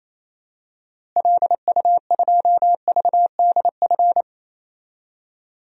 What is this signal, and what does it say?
Morse code, heard as a keyed tone, about 27 wpm, 700 Hz. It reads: LU2VDF
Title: DXCC-100 Call Signs - 6 Characters - Repeated 3x - 28wpm